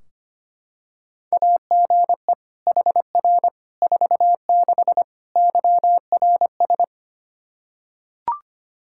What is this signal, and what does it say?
Morse code, heard as a keyed tone, about 25 wpm, 700 Hz. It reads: AGE HR 46 YRS
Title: K1USN SST QSO Practice 25wpm